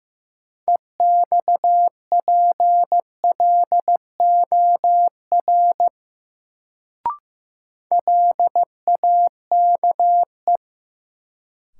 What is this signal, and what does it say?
Morse code, heard as a keyed tone, about 15 wpm, 700 Hz. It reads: EXPLOR LAKE
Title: Sentences from Top 1100 Words - New Word Drill Alt - 15wpm